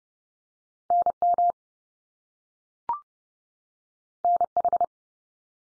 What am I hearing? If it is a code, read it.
Morse code, heard as a keyed tone, about 30 wpm, 700 Hz. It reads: NM NH
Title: US States 30WPM